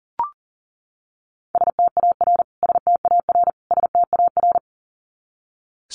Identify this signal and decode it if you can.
Morse code, heard as a keyed tone, about 40 wpm, 700 Hz. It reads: STAR STAR STAR
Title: Top 500 Words Repeated 40wpm